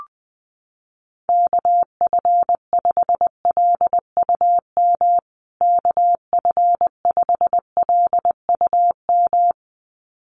Morse code, heard as a keyed tone, about 20 wpm, 700 Hz. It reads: KF5LUM KF5LUM
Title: USA Call Signs Repeated 2x 20wpm